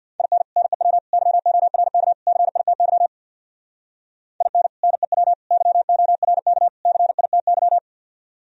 Morse code, heard as a keyed tone, about 58 wpm, 700 Hz. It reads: IN NEW YORK CITY IN NEW YORK CITY
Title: Speed-Racing - Sets of 4 Words 50wpm